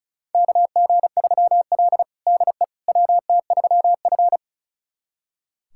Morse code, heard as a keyed tone, about 36 wpm, 700 Hz. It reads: KG3L DE WT3F
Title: USA Callsign DE Callsign - 4 Characters - 35wpm